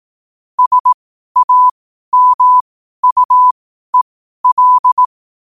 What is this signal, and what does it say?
Morse code, heard as a keyed tone, about 18 wpm, spaced as for 12 wpm, 1000 Hz. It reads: SAMUEL
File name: qotdfiles / 12wpm